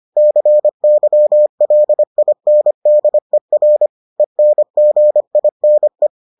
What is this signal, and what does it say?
Morse code, heard as a keyed tone, about 25 wpm, 600 Hz. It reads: CYLINDER ENGINE